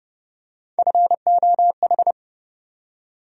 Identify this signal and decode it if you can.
Morse code, heard as a keyed tone, about 30 wpm, 700 Hz. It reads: FOH
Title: USA Call Sign Postfixes - 3 Letters 30wpm